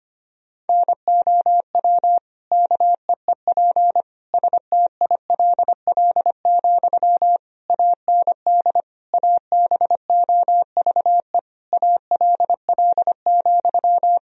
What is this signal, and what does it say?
Morse code, heard as a keyed tone, about 25 wpm, 700 Hz. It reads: NOW KEEP STILL, AND ABOVE ALL,
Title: Sentences from Top 1000 Words - Review Part 2 - 25wpm